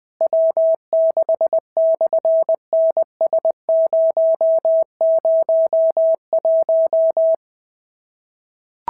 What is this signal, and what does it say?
Morse code, heard as a keyed tone, about 20 wpm, 650 Hz. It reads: W6/NS001
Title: SOTA QSO Practice 20wpm